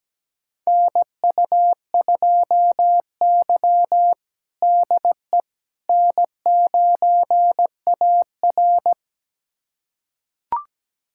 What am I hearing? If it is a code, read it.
Morse code, heard as a keyed tone, about 17 wpm, 700 Hz. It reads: NU2Y DE N9AR
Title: USA Callsign DE Callsign - 4 Characters - 17wpm